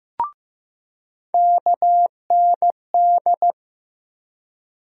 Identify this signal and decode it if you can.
Morse code, heard as a keyed tone, about 15 wpm, 700 Hz. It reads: KND
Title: USA Call Sign Postfixes - 3 Letters 15wpm